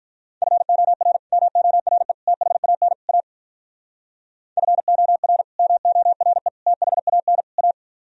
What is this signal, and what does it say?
Morse code, heard as a keyed tone, about 45 wpm, 700 Hz. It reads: FOR MORE THAN A FOR MORE THAN A